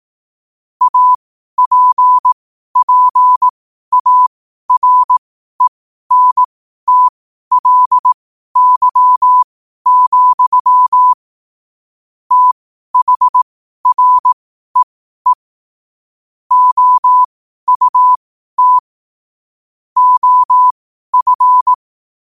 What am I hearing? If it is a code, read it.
Morse code, heard as a keyed tone, about 18 wpm, spaced as for 12 wpm, 1000 Hz. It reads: APPARENTLY, THREE OUT OF
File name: qotdfiles / 12wpm